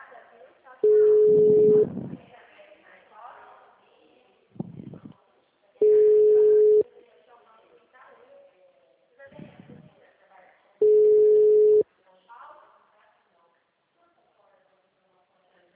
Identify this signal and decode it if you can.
Ringback tone (Europe, 425 Hz), with speech.